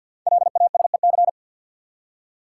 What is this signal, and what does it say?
Morse code, heard as a keyed tone, about 52 wpm, 700 Hz. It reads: LASEC
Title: Code Groups - 5 Letters - 50wpm